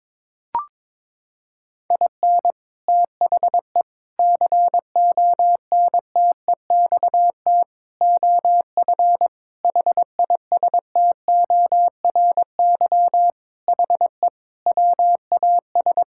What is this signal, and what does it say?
Morse code, heard as a keyed tone, about 22 wpm, 700 Hz. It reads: IN THE CONTEXT OF HISTORY HE WAS
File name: practice / 22wpm / Sentences with 4-Word Phrases - 22wpm